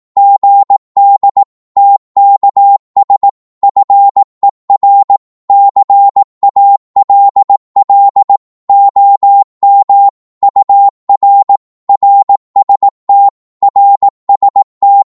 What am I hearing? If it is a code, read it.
Morse code, heard as a keyed tone, about 18 wpm, 800 Hz. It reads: GD TKS FER CALL OM UR RST RST
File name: Morse / qso-18wpm